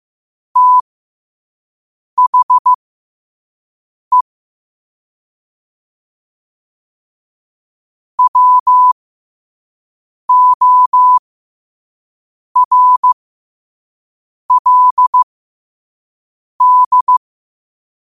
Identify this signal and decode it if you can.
Morse code, heard as a keyed tone, about 15 wpm, spaced as for 5 wpm, 1000 Hz. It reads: THE WORLD
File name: qotdfiles / 5wpm